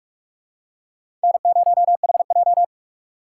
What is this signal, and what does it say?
Morse code, heard as a keyed tone, about 43 wpm, 700 Hz. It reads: N0HJ